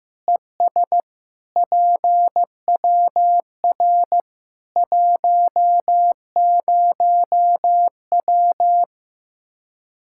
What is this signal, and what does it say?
Morse code, heard as a keyed tone, about 15 wpm, 700 Hz. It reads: ES PWR 10W
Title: K1USN SST QSO Practice 15wpm